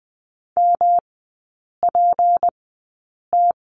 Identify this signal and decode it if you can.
Morse code, heard as a keyed tone, about 20 wpm, spaced as for 8 wpm, 690 Hz. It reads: MPT
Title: 212 Lesson 34e - Number 8 - Three Characters 20wpm